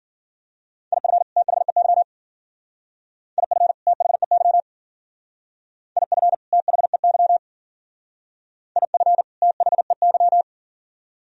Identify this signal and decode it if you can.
Morse code, heard as a keyed tone, about 52 wpm, 700 Hz. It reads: IF THEY IF THEY IF THEY IF THEY